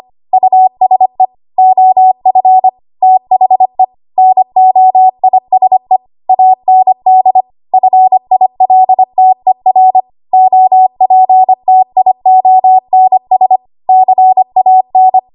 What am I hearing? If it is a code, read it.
Morse code, heard as a keyed tone, about 25 wpm, 750 Hz. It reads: USE OF THE NOISE AND FILTER OPTIONS CAN
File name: Archive / 25 WPM / 25WPM